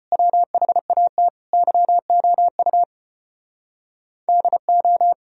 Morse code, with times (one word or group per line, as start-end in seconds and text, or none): 0.1-1.3 WHAT
1.5-2.8 YOU
4.3-5.1 DO